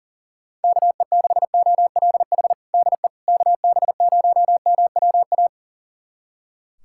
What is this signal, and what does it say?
Morse code, heard as a keyed tone, about 40 wpm, 700 Hz. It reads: KE6OLH DE KB0MWA
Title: USA Callsign DE Callsign - 6 Characters - 40wpm